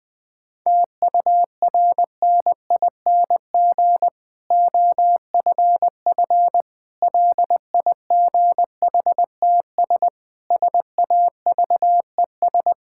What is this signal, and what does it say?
Morse code, heard as a keyed tone, about 20 wpm, 700 Hz. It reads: TURNING OFF LIGHTS SAVES
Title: Difficult Sentences with Hints - 20wpm